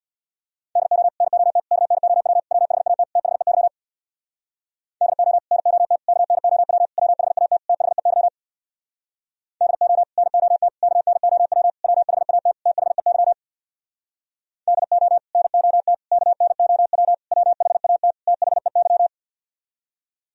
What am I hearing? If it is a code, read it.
Morse code, heard as a keyed tone, about 64 wpm, 700 Hz. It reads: DO NOT KNOW WHAT THEY DO NOT KNOW WHAT THEY DO NOT KNOW WHAT THEY DO NOT KNOW WHAT THEY